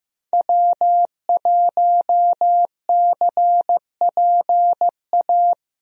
Morse code, heard as a keyed tone, about 15 wpm, 700 Hz. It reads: W1CPA
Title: USA Call Signs - 5 Characters - Repeated 2x - 15wpm